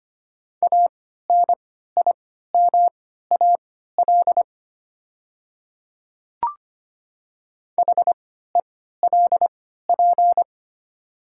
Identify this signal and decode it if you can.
Morse code, heard as a keyed tone, about 25 wpm, spaced as for 14 wpm, 700 Hz. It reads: ANIMAL HELP
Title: 3x Character Spacing - Top 200 Words - 25wpm